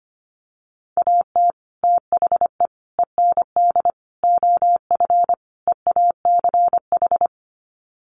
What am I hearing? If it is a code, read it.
Morse code, heard as a keyed tone, about 25 wpm, 700 Hz. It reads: AT THE END OF EACH
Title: Sets of 5 Words Encore 25wpm